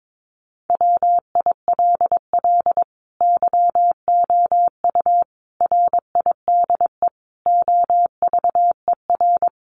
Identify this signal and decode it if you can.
Morse code, heard as a keyed tone, about 22 wpm, 700 Hz. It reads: WILL YOU RIDE OVER
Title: Sentences from Top 1200 Words 22wpm